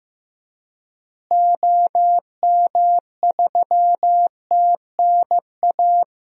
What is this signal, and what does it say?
Morse code, heard as a keyed tone, about 15 wpm, 700 Hz. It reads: OM3TNA
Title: DXCC-100 Call Signs - 6 Characters - Repeated 3x - 15wpm